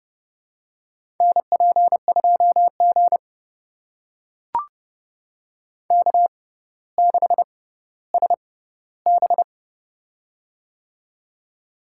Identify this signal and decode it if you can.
Morse code, heard as a keyed tone, about 32 wpm, spaced as for 16 wpm, 700 Hz. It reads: NP2G K6SB
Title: USA Call Signs - 4 Characters - 6x Character Spacing - 30wpm